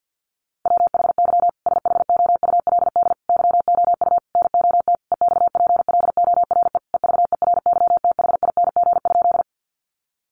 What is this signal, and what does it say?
Morse code, heard as a keyed tone, about 48 wpm, 700 Hz. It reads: WHY SHOULD YOU NOT EXPLORE EVERYTHING?
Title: Sentences from Top 1100 Words 50wpm